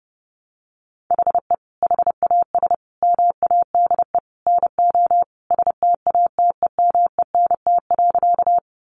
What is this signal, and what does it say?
Morse code, heard as a keyed tone, about 30 wpm, 700 Hz. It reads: HE HAS MADE NO STATEMENT.
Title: Sentences from Top 1200 Words 30wpm